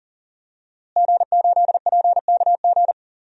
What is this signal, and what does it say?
Morse code, heard as a keyed tone, about 37 wpm, 700 Hz. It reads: G8PKG
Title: DXCC-100 Call Signs - 5 Characters - Repeated 3x - 40wpm